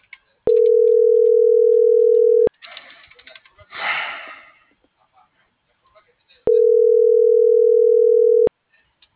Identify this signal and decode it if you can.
Ringback tone (North America), 440 + 480 Hz.
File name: CENTRAL MAYAB / out vianey.prado